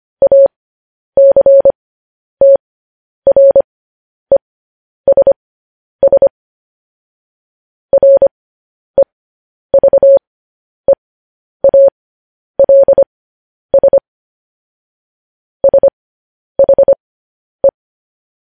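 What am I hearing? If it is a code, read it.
Morse code, heard as a keyed tone, about 25 wpm, spaced as for 10 wpm, 550 Hz. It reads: ACTRESS REVEALS SHE
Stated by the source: Morse code news headlines practice, at 10 WPM.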